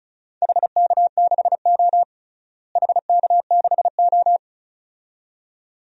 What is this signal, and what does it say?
Morse code, heard as a keyed tone, about 36 wpm, 700 Hz. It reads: HK6O HK6O
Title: DXCC-100 Call Signs - 4 Characters - Repeated 2x - 35wpm